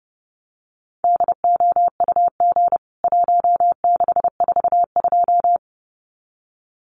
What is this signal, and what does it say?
Morse code, heard as a keyed tone, about 30 wpm, 700 Hz. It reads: DOUG 1642
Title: CWT Exchange 30wpm